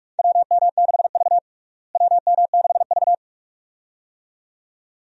Morse code, heard as a keyed tone, about 44 wpm, 700 Hz. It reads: WM6V WM6V
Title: USA Call Signs - 4 Characters - Repeated 2x - 45wpm